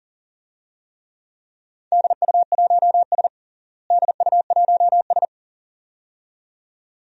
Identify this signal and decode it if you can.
Morse code, heard as a keyed tone, about 40 wpm, 700 Hz. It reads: DU1S DU1S
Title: DXCC-100 Call Signs - 4 Characters - Repeated 2x - 40wpm